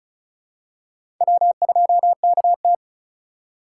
Morse code, heard as a keyed tone, about 34 wpm, 700 Hz. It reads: W2KT
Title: USA Call Signs - 4 Characters - 3x Character Spacing - 35wpm